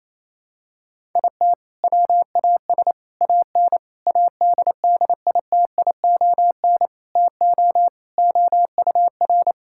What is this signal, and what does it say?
Morse code, heard as a keyed tone, about 28 wpm, 700 Hz. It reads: IT WAS AN ADDITION TO OUR
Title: Sentences from Top 1000 Words - Review Part 6 - 28wpm